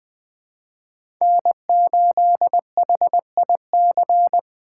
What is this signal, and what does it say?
Morse code, heard as a keyed tone, about 20 wpm, 700 Hz. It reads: N8HIC
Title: USA Call Signs - 5 Characters - 4x Character Spacing - 20wpm